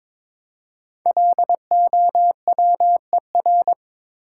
Morse code, heard as a keyed tone, about 22 wpm, 690 Hz. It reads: LOWER
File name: 5 Letter Words - First 3 Letters Given - 22wpm